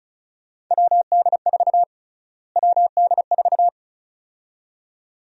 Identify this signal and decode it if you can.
Morse code, heard as a keyed tone, about 36 wpm, 700 Hz. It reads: WD4 WD4
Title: USA Call Sign Prefixes - Repeated 2x - 35wpm